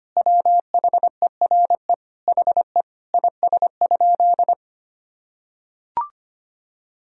Morse code, heard as a keyed tone, about 25 wpm, 700 Hz. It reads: WHERE HE IS?
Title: Sentences from Top 300 Words and Abbreviations 25wpm